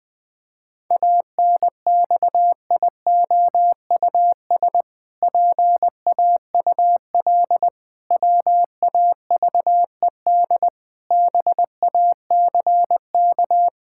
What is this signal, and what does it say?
Morse code, heard as a keyed tone, about 20 wpm, 700 Hz. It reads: ANXIOUS PAUL WAVED BACK
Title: Pangram Challenge v2 - 20wpm